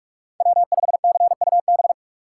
Morse code, heard as a keyed tone, about 43 wpm, 700 Hz. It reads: W5CUB
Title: USA Call Signs - 5 Characters - 3x Character Spacing - 45wpm